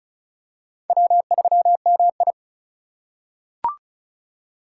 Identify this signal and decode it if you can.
Morse code, heard as a keyed tone, about 34 wpm, 700 Hz. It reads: W3MI
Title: USA Call Signs - 4 Characters - 2x Character Spacing - 35wpm